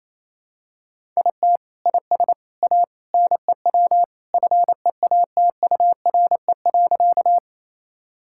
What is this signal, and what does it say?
Morse code, heard as a keyed tone, about 28 wpm, 700 Hz. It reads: IT IS A NEW FEATURE.
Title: Sentences from Top 1000 Words - Review Part 5 - 28wpm